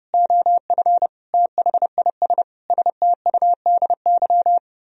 Morse code, heard as a keyed tone, about 33 wpm, 700 Hz. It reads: OF THIS STUDY